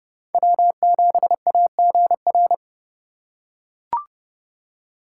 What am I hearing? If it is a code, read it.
Morse code, heard as a keyed tone, about 30 wpm, 700 Hz. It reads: W7AGR
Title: USA Call Signs - 5 Characters - 4x Character Spacing - 30wpm